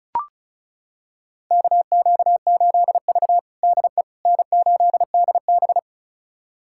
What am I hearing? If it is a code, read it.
Morse code, heard as a keyed tone, about 36 wpm, 700 Hz. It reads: KQ8V DE N8DB
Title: USA Callsign DE Callsign - 4 Characters - 35wpm